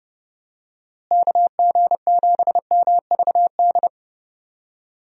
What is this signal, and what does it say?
Morse code, heard as a keyed tone, about 30 wpm, 700 Hz. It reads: KG7MVD
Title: USA Call Signs - 6 Characters - Repeated 2x - 30wpm